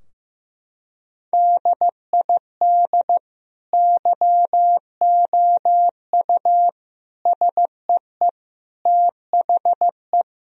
Morse code, heard as a keyed tone, about 15 wpm, 700 Hz. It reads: DID YOU SEE THE
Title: Sentences from Top 1000 Words - Review Part 2 - 15wpm